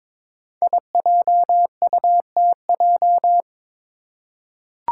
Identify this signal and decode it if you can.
Morse code, heard as a keyed tone, about 22 wpm, 700 Hz. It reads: IJUTJ
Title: Five Challenging Letter-Number Combinations 22wpm